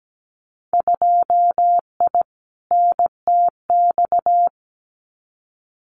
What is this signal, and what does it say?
Morse code, heard as a keyed tone, about 17 wpm, 700 Hz. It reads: 2I NTX
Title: Winter Field Day Exchanges Encore 17wpm